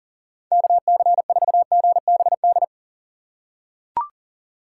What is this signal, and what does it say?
Morse code, heard as a keyed tone, about 40 wpm, 700 Hz. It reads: KC4GBD
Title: USA Call Signs - 6 Characters - 4x Character Spacing - 40wpm